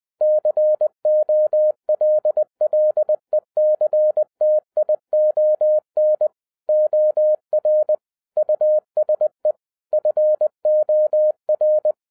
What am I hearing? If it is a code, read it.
Morse code, heard as a keyed tone, about 20 wpm, 610 Hz. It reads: COLLECTION OR USE FOR
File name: 2011-07-mp3 / n43 Czech Army Morse Keys 20wpm